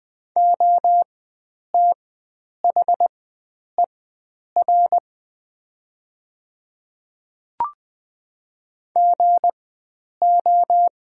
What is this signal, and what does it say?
Morse code, heard as a keyed tone, about 20 wpm, spaced as for 9 wpm, 700 Hz. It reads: OTHER GO
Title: 4x Character Spacing - Top 300 Words - 20wpm